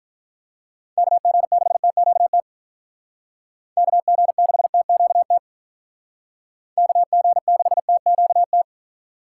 Morse code, heard as a keyed tone, about 47 wpm, 700 Hz. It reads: KG6TQT KG6TQT KG6TQT